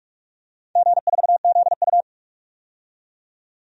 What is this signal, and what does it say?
Morse code, heard as a keyed tone, about 41 wpm, 700 Hz. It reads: G4ZU
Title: DXCC-100 Call Signs - 4 Characters - 45wpm